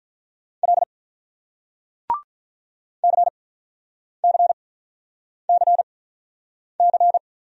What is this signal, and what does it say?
Morse code, heard as a keyed tone, about 47 wpm, spaced as for 9 wpm, 700 Hz. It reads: L CCCC